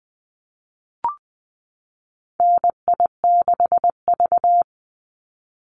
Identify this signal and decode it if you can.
Morse code, heard as a keyed tone, about 20 wpm, 700 Hz. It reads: NI6V
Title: Call Signs - 20wpm